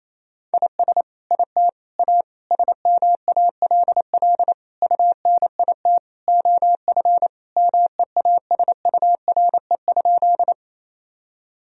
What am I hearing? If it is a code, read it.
Morse code, heard as a keyed tone, about 27 wpm, 690 Hz. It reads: IS IT A SMALL UNIT OF MEASURE?
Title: Sentences from Top 1000 Words - Review Part 2 - 28wpm